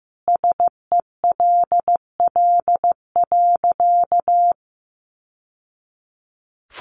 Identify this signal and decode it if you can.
Morse code, heard as a keyed tone, about 14 wpm, 700 Hz. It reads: SELL.
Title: Sentences from Top 1100 Words 15wpm